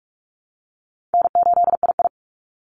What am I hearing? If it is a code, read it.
Morse code, heard as a keyed tone, about 46 wpm, 700 Hz. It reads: N8II
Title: Easy Call Signs - 45wpm